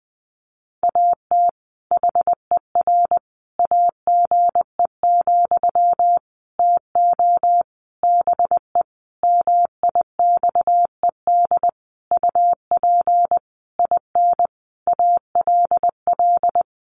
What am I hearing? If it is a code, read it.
Morse code, heard as a keyed tone, about 20 wpm, 700 Hz. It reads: AT HER AGE, TO BE MIXED UP IN ALL
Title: Sentences from Top 1500 Words 20wpm